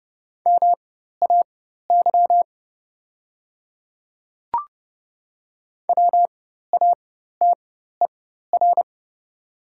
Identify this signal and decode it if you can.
Morse code, heard as a keyed tone, about 29 wpm, spaced as for 14 wpm, 700 Hz. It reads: MAY WATER
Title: 4x Character Spacing - Top 100 Words - 30wpm